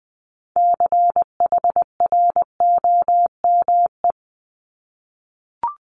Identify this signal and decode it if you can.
Morse code, heard as a keyed tone, about 20 wpm, 700 Hz. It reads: CHROME
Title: Multi-letter Phonemes 20wpm